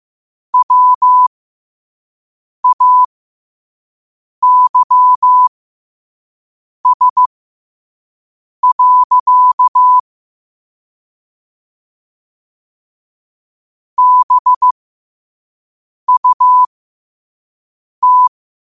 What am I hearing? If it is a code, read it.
Morse code, heard as a keyed tone, about 15 wpm, spaced as for 5 wpm, 1000 Hz. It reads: WAYS. BUT
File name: qotdfiles / 5wpm